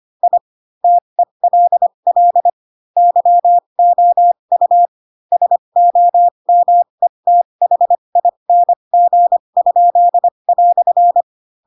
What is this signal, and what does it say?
Morse code, heard as a keyed tone, about 25 wpm, spaced as for 22 wpm, 700 Hz. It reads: I TELL YOU SOMETHING?"
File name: Pinocchio / Pinocchio ch4 18 25 ik1hge